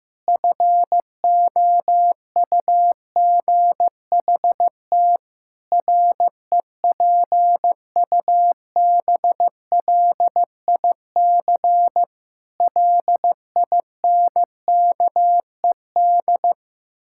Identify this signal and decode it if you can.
Morse code, heard as a keyed tone, about 14 wpm, 700 Hz. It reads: FOUGHT REPUBLIC LINKED
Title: Pangram Challenge v2 - 15wpm